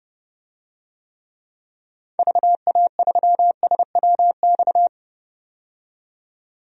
Morse code, heard as a keyed tone, about 30 wpm, 700 Hz. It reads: VA3SWX
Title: DXCC-100 Call Signs - 6 Characters - 30wpm